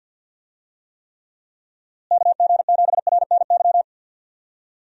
Morse code, heard as a keyed tone, about 52 wpm, 700 Hz. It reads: KG7RNY
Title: USA Call Signs - 6 Characters - Repeated 3x - 50wpm